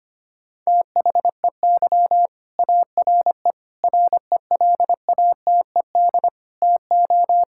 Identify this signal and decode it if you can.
Morse code, heard as a keyed tone, about 25 wpm, 700 Hz. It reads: THEY ARE RELATED TO
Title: Sentences from Top 1000 Words - Review Part 5 - 25wpm